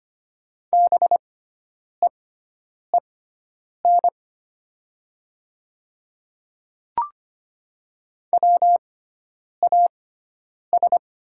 Morse code, heard as a keyed tone, about 25 wpm, spaced as for 9 wpm, 700 Hz. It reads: BEEN WAS
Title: 6x Character Spacing - Top 200 Words - 25wpm